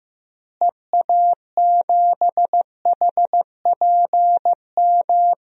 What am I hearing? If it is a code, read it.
Morse code, heard as a keyed tone, about 16 wpm, 700 Hz. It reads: EA7HPM
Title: DXCC-100 Call Signs - 6 Characters - Repeated 3x - 15wpm